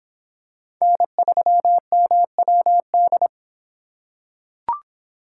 Morse code, heard as a keyed tone, about 26 wpm, 700 Hz. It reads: N3MWD